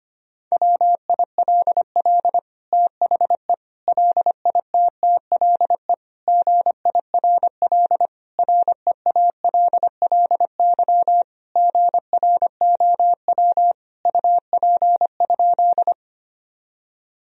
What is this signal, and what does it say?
Morse code, heard as a keyed tone, about 25 wpm, 700 Hz. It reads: WILL THE LITTLE GIRL REALLY GROW UP?
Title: Sentences from Top 1000 Words - Review Part 1 - 25wpm